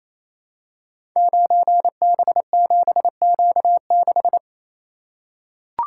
Morse code, heard as a keyed tone, about 27 wpm, 700 Hz. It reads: 9B7Q6
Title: Code Groups - 5 Letter:Numbers - 28wpm